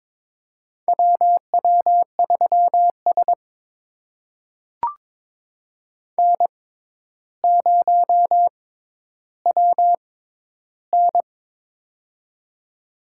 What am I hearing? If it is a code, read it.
Morse code, heard as a keyed tone, about 22 wpm, spaced as for 11 wpm, 700 Hz. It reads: WW3S N0WN
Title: USA Call Signs - 4 Characters - 6x Character Spacing - 22wpm